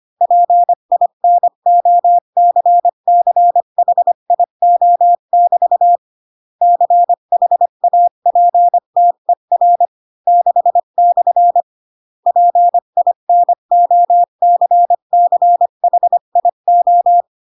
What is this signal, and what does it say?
Morse code, heard as a keyed tone, about 25 wpm, 700 Hz. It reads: PINOCCHIO<BT> CHAPTER 6/ PINOCCHIO
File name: Pinocchio / Pinocchio ch6 20 25 ik1hge